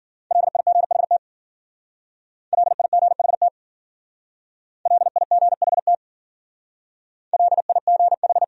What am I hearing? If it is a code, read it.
Morse code, heard as a keyed tone, about 50 wpm, 700 Hz. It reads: LIGHT LIGHT LIGHT LIGH
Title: Speed-Racing Top 500 Words 40wpm